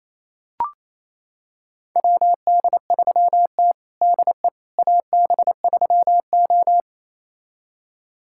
Morse code, heard as a keyed tone, about 28 wpm, 700 Hz. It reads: WD3T DE AB3O
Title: USA Callsign DE Callsign - 4 Characters - 28wpm